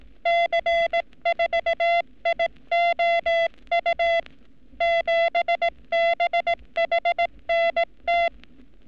Morse code, heard as a keyed tone, about 18 wpm, 680 Hz. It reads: C4IOU 7BHNT
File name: training / NAVPERS 11224-11235 / NAVPERS 11233 RA RB